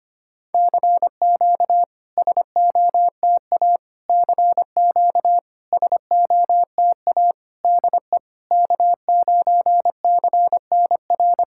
Morse code, heard as a keyed tone, about 25 wpm, 700 Hz. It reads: CQ SOTA CQ SOTA DE K9CNR
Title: SOTA QSO Practice 25wpm